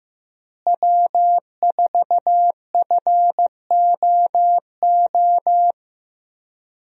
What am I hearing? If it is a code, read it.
Morse code, heard as a keyed tone, about 15 wpm, 700 Hz. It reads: W4FOO
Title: USA Call Signs - 5 Characters - Repeated 3x - 15wpm